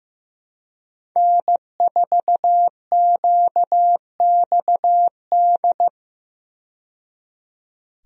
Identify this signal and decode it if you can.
Morse code, heard as a keyed tone, about 15 wpm, 700 Hz. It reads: N4QXD
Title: USA Call Signs - 5 Characters - 15wpm